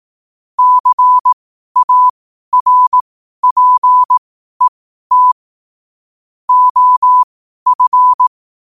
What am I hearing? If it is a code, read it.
Morse code, heard as a keyed tone, about 18 wpm, spaced as for 12 wpm, 1000 Hz. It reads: CARPET OF
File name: qotdfiles / 12wpm